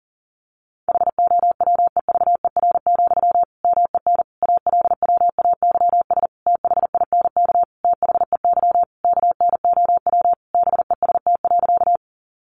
Morse code, heard as a keyed tone, about 40 wpm, 700 Hz. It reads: HOWEVER, MEN ALWAYS THINK THEY KNOW BEST.
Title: Sentences from Top 600 Words 40wpm